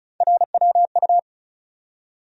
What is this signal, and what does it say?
Morse code, heard as a keyed tone, about 36 wpm, 700 Hz. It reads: RWU
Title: USA Call Sign Postfixes - 3 Letters 35wpm